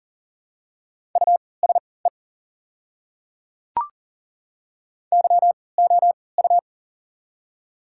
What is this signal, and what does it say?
Morse code, heard as a keyed tone, about 37 wpm, spaced as for 22 wpm, 690 Hz. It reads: USE YOU
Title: 3x Character Spacing - Top 100 Words - 40wpm